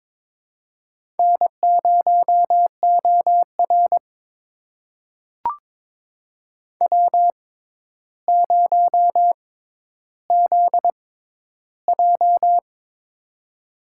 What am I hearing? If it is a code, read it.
Morse code, heard as a keyed tone, about 22 wpm, spaced as for 11 wpm, 700 Hz. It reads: N0OR W0ZJ
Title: USA Call Signs - 4 Characters - 6x Character Spacing - 22wpm